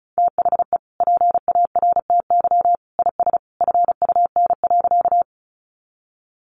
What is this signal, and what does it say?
Morse code, heard as a keyed tone, about 34 wpm, 700 Hz. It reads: THE PARTY IS FUN.
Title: Easy Phrases 35wpm